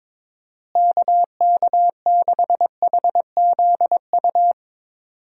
Morse code, heard as a keyed tone, about 22 wpm, 700 Hz. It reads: KK6HZU
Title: USA Call Signs - 6 Characters - Repeated 3x - 22wpm